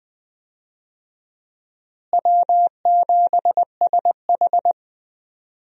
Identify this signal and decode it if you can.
Morse code, heard as a keyed tone, about 20 wpm, 700 Hz. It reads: W7SH
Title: USA Call Signs - 4 Characters - Repeated 3x - 20wpm